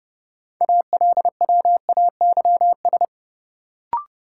Morse code, heard as a keyed tone, about 30 wpm, 700 Hz. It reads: ALWAYS
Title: Sentences from Top 300 Words - New Word Drill - 30wpm